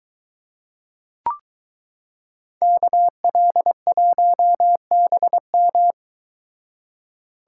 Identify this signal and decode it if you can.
Morse code, heard as a keyed tone, about 22 wpm, 700 Hz. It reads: KL1BM